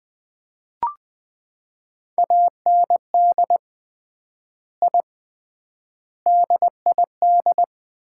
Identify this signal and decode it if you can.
Morse code, heard as a keyed tone, about 21 wpm, 700 Hz. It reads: AND I DID
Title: 3x Word-Spacing - Sets of 5 Words - 20wpm